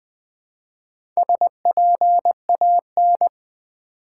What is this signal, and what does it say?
Morse code, heard as a keyed tone, about 20 wpm, 700 Hz. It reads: SPAN
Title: Build A Word - 1 Letter At A Time - 3 to 5-Letter Words - 20wpm